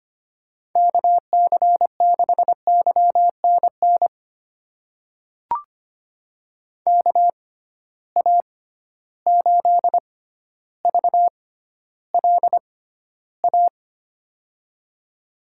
Morse code, heard as a keyed tone, about 25 wpm, spaced as for 13 wpm, 700 Hz. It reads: KC6YNN KA8VLA
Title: USA Call Signs - 6 Characters - 6x Character Spacing - 25wpm